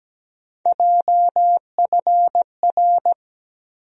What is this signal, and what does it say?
Morse code, heard as a keyed tone, about 17 wpm, 700 Hz. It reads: JFR
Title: USA Call Sign Postfixes - 3 Letters 17wpm